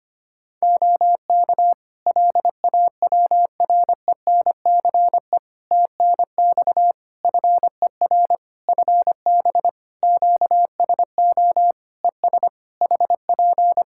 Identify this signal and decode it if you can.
Morse code, heard as a keyed tone, about 25 wpm, 700 Hz. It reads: OK LAWRENCE TNX FER FB QSO ES HP
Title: Practice QSO - Exchange 4 - Ending - 25wpm